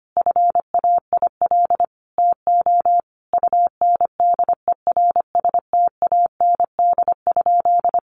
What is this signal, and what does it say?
Morse code, heard as a keyed tone, about 25 wpm, 700 Hz. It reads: FAIL TO UNDERSTAND?
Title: Sentences from Top 1100 Words 25wpm